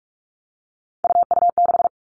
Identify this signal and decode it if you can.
Morse code, heard as a keyed tone, about 46 wpm, 700 Hz. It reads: UU6
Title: Three Challenging Letter-Number Combinations 45wpm